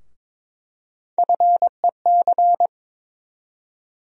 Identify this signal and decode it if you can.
Morse code, heard as a keyed tone, about 22 wpm, 700 Hz. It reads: FEC
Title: USA Call Sign Postfixes - 3 Letters 22wpm